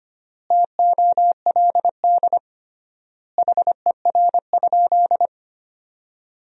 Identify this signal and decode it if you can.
Morse code, heard as a keyed tone, about 25 wpm, 700 Hz. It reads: TOLD HER?
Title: 3x Word-Spacing - Sentences from Top 400 Words - 25wpm